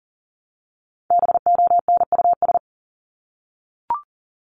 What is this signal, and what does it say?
Morse code, heard as a keyed tone, about 38 wpm, 700 Hz. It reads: BONUS
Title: Top 1751-2000 Words 40wpm